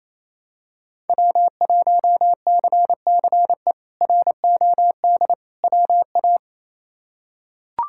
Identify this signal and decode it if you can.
Morse code, heard as a keyed tone, about 28 wpm, 700 Hz. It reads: W1CCE ROD WA
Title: K1USN SST QSO Practice - 28wpm